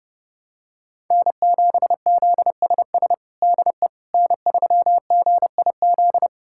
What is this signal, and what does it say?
Morse code, heard as a keyed tone, about 30 wpm, 700 Hz. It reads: N7ZSS DE N3GIZ
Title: USA Callsign DE Callsign - 5 Characters - 30wpm